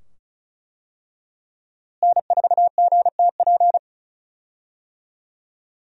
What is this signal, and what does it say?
Morse code, heard as a keyed tone, about 36 wpm, 700 Hz. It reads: N4GTP
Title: USA Call Signs - 5 Characters - Repeated 2x - 35wpm